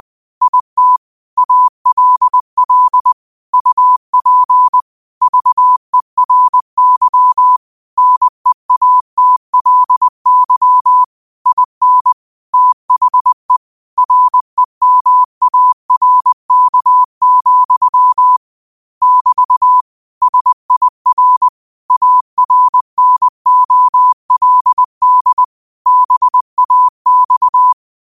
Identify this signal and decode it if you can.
Morse code, heard as a keyed tone, about 20 wpm, 1000 Hz. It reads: IT ALL UP VERY NEATLY IN THE REMARK, <BT> SIR ARNOLD BAX